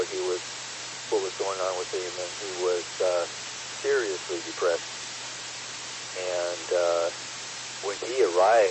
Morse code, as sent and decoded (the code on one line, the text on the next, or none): none